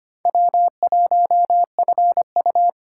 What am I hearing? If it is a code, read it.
Morse code, heard as a keyed tone, about 25 wpm, 700 Hz. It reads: W1FU
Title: USA Callsign DE Callsign 25wpm